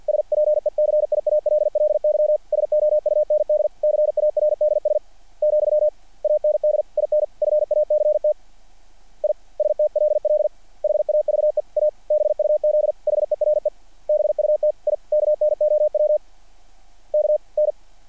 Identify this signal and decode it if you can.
Morse code, heard as a keyed tone, about 50 wpm, 600 Hz. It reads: SPECIALLY SOUND CARDS , AND IN FACT I STLL HAVE A BUZ HERE BUT I KNOW KN
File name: archived / CRSnet